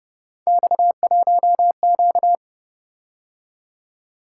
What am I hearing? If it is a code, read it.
Morse code, heard as a keyed tone, about 28 wpm, 700 Hz. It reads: X1Q